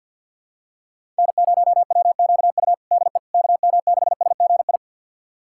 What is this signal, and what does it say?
Morse code, heard as a keyed tone, about 51 wpm, 690 Hz. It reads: N0WQU DE KM6SGI
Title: USA Callsign DE Callsign 50wpm